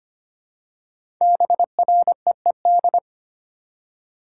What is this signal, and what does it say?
Morse code, heard as a keyed tone, about 25 wpm, 700 Hz. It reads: BREED
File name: 5 Letter Words - First 3 Letters Given - 25wpm